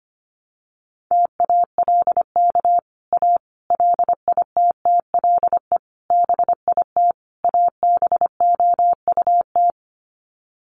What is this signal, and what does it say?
Morse code, heard as a keyed tone, about 25 wpm, 700 Hz. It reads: TALK A LITTLE BIT ABOUT
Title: Sets of 5 Words Encore 25wpm